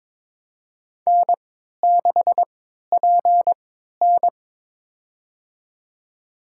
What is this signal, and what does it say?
Morse code, heard as a keyed tone, about 22 wpm, spaced as for 13 wpm, 700 Hz. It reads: N6PN
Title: USA Call Signs - 4 Characters - 3x Character Spacing - 22wpm